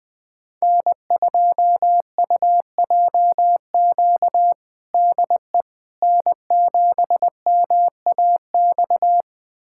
Morse code, heard as a keyed tone, about 21 wpm, 700 Hz. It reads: N2UJQ DE N7MAX
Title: USA Callsign DE Callsign - 5 Characters - 20wpm